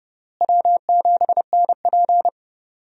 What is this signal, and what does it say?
Morse code, heard as a keyed tone, about 32 wpm, 700 Hz. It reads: W7NP
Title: USA Call Signs - 4 Characters - Repeated 3x - 30wpm